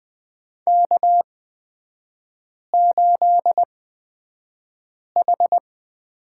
Morse code, heard as a keyed tone, about 20 wpm, spaced as for 5 wpm, 700 Hz. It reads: K8H
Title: 255 Lesson 41e - Period - Three Characters 20wpm